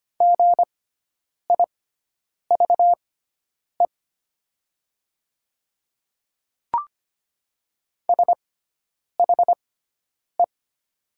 Morse code, heard as a keyed tone, about 25 wpm, spaced as for 9 wpm, 700 Hz. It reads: GIVE SHE
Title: 6x Character Spacing - Top 300 Words - 25wpm